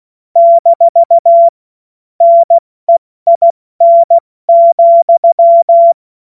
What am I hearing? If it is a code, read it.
Morse code, heard as a keyed tone, about 16 wpm, 680 Hz. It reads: - NEIN,
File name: Morsekurs / Witze / Witz 31.Oktober 2023 70 F80